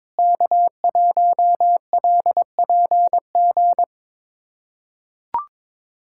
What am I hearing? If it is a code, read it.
Morse code, heard as a keyed tone, about 22 wpm, 700 Hz. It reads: K1LPG
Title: USA Callsign DE Callsign - 5 Characters - 22wpm